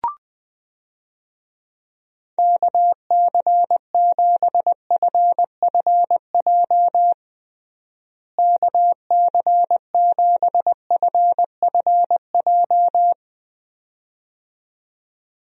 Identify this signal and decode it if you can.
Morse code, heard as a keyed tone, about 20 wpm, 700 Hz. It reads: KC7FFJ KC7FFJ
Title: USA Call Signs - 6 Characters - Repeated 2x - 20wpm